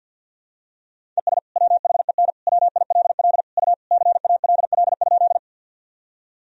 Morse code, heard as a keyed tone, about 50 wpm, 700 Hz. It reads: ES WHEN WILL U CALL?
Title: Sentences from Top 100 Words and Abbreviations 50wpm